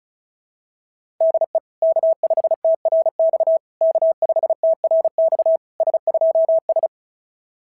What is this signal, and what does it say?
Morse code, heard as a keyed tone, about 36 wpm, 690 Hz. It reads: DE K5TRX K5TRX S2S
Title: SOTA QSO Practice 35wpm